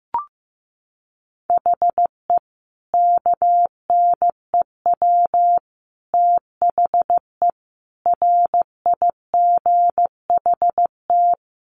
Morse code, heard as a keyed tone, about 15 wpm, 700 Hz. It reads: HE KNEW THE RIGHT
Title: Sentences from Top 1300 Words 15wpm